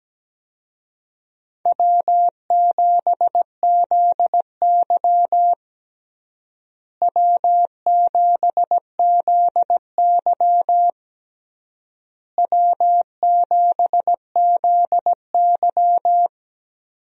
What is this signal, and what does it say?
Morse code, heard as a keyed tone, about 17 wpm, 700 Hz. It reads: W7ZY W7ZY W7ZY
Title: USA Call Signs - 4 Characters - Repeated 3x - 17wpm